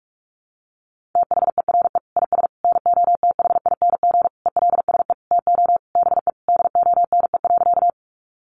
Morse code, heard as a keyed tone, about 44 wpm, 700 Hz. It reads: THERE IS NOTHING ELSE TO BE DONE.
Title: Sentences from Top 800 Words 45wpm